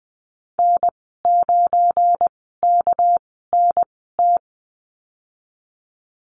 Morse code, heard as a keyed tone, about 20 wpm, spaced as for 14 wpm, 700 Hz. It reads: N9KNT
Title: 207 Lesson 33g - Letter j - Call Signs 20wpm